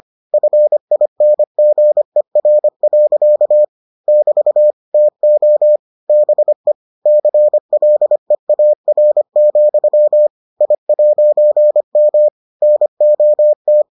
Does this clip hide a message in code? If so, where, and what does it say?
Yes: Morse code from 0.3 to 13.8 s, heard as a keyed tone, about 25 wpm, 600 Hz. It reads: FINGER. <BT> TO BE CLEAR, I'M NOT